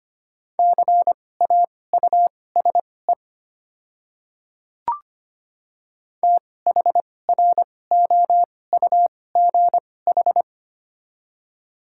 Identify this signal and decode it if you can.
Morse code, heard as a keyed tone, about 25 wpm, spaced as for 18 wpm, 700 Hz. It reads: CAUSE THROUGH
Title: 2x Character Spacing - Top 300 Words - 25wpm